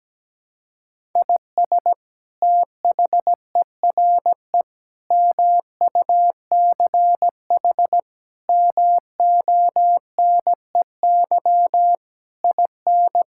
Morse code, heard as a keyed tone, about 17 wpm, 700 Hz. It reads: IS THERE MUCH MONEY IN
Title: Sentences from Top 1000 Words - Review Part 5 - 17wpm